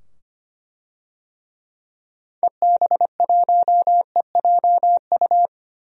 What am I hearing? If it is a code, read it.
Morse code, heard as a keyed tone, about 25 wpm, 700 Hz. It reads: EB1EJU
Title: DXCC-100 Call Signs - 6 Characters - 25wpm